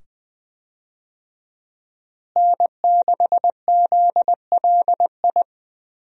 Morse code, heard as a keyed tone, about 20 wpm, 700 Hz. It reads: N6ZLI
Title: USA Call Signs - 5 Characters - Repeated 2x - 20wpm